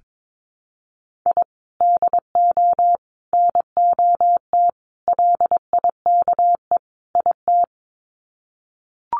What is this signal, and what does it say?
Morse code, heard as a keyed tone, about 22 wpm, 700 Hz. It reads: I DO NOT LIKE IT
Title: Sets of 5 Words Encore 22wpm